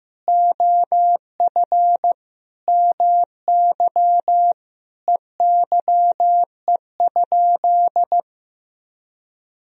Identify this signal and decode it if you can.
Morse code, heard as a keyed tone, about 15 wpm, 700 Hz. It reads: OF MY EYE?
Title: Sentences from Top 500 Words and Abbreviations 15wpm